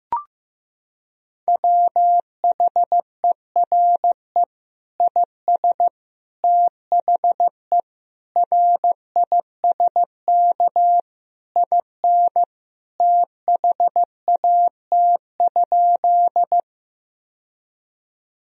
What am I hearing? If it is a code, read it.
Morse code, heard as a keyed tone, about 15 wpm, 700 Hz. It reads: WHERE IS THE RISK IN THAT?
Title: Sentences from Top 1000 Words - Review Part 4 - 15wpm